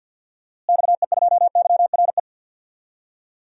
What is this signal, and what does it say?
Morse code, heard as a keyed tone, about 49 wpm, 700 Hz. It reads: XE2YRE
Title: DXCC-100 Call Signs - 6 Characters - Repeated 2x - 50wpm